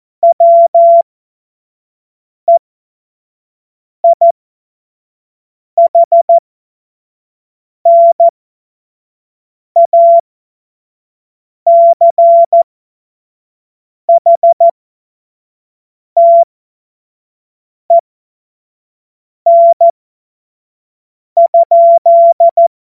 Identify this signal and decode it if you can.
Morse code, heard as a keyed tone, about 14 wpm, spaced as for 5 wpm, 680 Hz. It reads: WEIHNACHTEN?
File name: Morsekurs / Witze / Witz 26. Oktober 2021, 25 F70